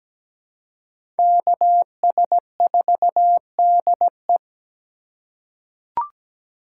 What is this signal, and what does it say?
Morse code, heard as a keyed tone, about 17 wpm, 700 Hz. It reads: KS4DE
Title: USA Call Signs - 5 Characters - 4x Character Spacing - 17wpm